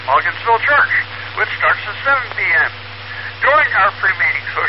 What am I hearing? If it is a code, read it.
Emergency Alert System (SAME) tones, heard as harsh buzzing bursts, with speech. It is heard from 0.7 to 4.5 s.